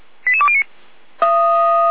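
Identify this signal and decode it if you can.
Five-tone selective call (ZVEI), heard as quick runs of five tones, tones of about 70 ms. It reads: 88209
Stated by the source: ZVEI five-tone selective call received by SDR.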